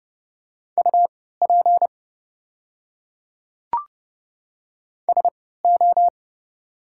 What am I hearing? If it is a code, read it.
Morse code, heard as a keyed tone, about 30 wpm, spaced as for 17 wpm, 700 Hz. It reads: UP SO
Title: 3x Character Spacing - Top 100 Words - 30wpm